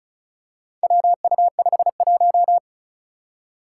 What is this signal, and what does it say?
Morse code, heard as a keyed tone, about 36 wpm, 700 Hz. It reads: WU51
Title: Four Challenging Letter-Number Combinations 35wpm